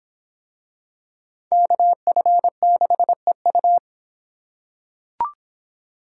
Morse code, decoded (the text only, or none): KF6EU